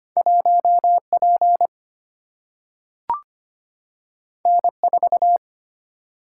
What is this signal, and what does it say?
Morse code, heard as a keyed tone, about 25 wpm, 700 Hz. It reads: 1 P N 4
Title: Single-Letter Single-Number 25wpm